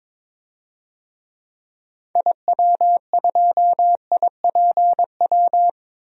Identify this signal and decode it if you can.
Morse code, heard as a keyed tone, about 23 wpm, 700 Hz. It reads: IW2IPW
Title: DXCC-100 Call Signs - 6 Characters - 22wpm